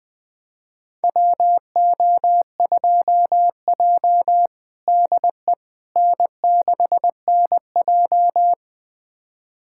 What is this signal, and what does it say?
Morse code, heard as a keyed tone, about 20 wpm, 700 Hz. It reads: WO2J DE N6NJ
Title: USA Callsign DE Callsign - 4 Characters - 20wpm